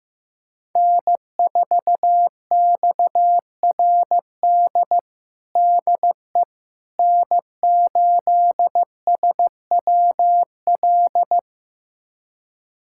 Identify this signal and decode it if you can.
Morse code, heard as a keyed tone, about 15 wpm, 700 Hz. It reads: N4XRD DE N8SWL
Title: USA Callsign DE Callsign 15wpm